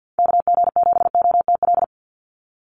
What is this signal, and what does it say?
Morse code, heard as a keyed tone, about 48 wpm, 700 Hz. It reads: KG7OTL
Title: USA Call Signs 50wpm